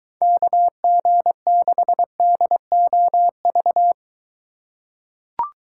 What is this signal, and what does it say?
Morse code, heard as a keyed tone, about 23 wpm, 700 Hz. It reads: KG6DOV